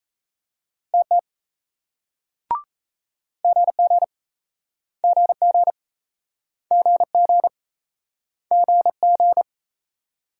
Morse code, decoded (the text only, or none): TT GG GG GG GG